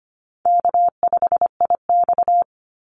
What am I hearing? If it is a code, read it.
Morse code, heard as a keyed tone, about 25 wpm, 700 Hz. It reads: K5IX
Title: Easy Call Signs Repeated 2x - 25wpm